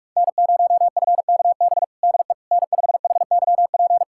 Morse code, heard as a keyed tone, about 46 wpm, 700 Hz. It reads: N0FKB DE N5HYP
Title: USA Callsign DE Callsign - 5 Characters - 45wpm